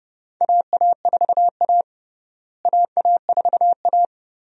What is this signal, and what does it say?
Morse code, heard as a keyed tone, about 32 wpm, 700 Hz. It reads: AA4A AA4A
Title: USA Call Signs - 4 Characters - Repeated 2x - 30wpm